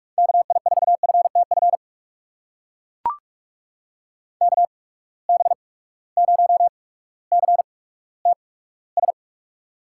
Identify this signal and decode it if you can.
Morse code, heard as a keyed tone, about 43 wpm, spaced as for 19 wpm, 700 Hz. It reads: KI4FTF KB0CTS
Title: USA Call Signs - 6 Characters - 8x Character Spacing - 45wpm